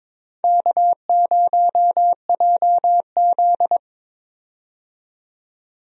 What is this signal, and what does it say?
Morse code, heard as a keyed tone, about 22 wpm, 700 Hz. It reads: K0JZ
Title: USA Call Signs - 4 Characters - 22wpm